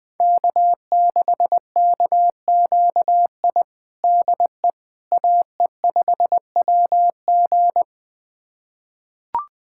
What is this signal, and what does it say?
Morse code, heard as a keyed tone, about 21 wpm, 700 Hz. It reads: K6KQI DE AE5WG
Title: USA Callsign DE Callsign - 5 Characters - 20wpm